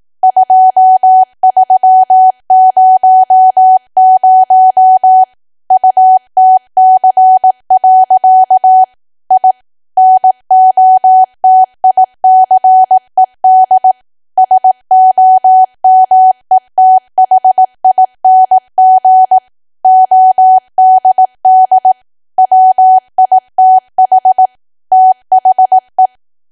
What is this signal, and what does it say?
Morse code, heard as a keyed tone, about 18 wpm, 750 Hz. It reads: 2300 UTC. I NOTICED SOMETHING ODD WITH THE
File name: cwa / W1AW-Level2 / 18WPM